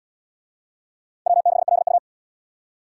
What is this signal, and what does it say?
Morse code, heard as a keyed tone, about 70 wpm, 700 Hz. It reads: W6LU